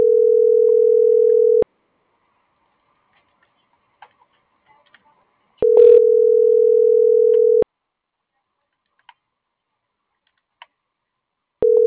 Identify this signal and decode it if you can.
Ringback tone (North America), 440 + 480 Hz.
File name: CENTRAL GENERAL / out karla.sky - .14096